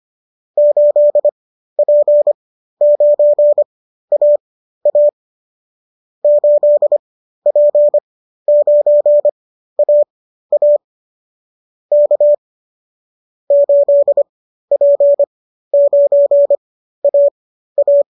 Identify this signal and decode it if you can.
Morse code, heard as a keyed tone, about 25 wpm, spaced as for 13 wpm, 590 Hz. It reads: 8P9AA 8P9AA K 8P9AA